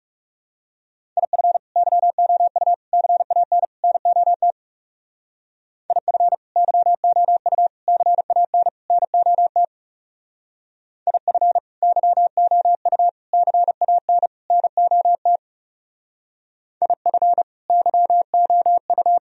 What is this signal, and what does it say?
Morse code, heard as a keyed tone, about 37 wpm, 700 Hz. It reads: IF YOU CAN NOT IF YOU CAN NOT IF YOU CAN NOT IF YOU
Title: Speed-Racing - Sets of 4 Words 30wpm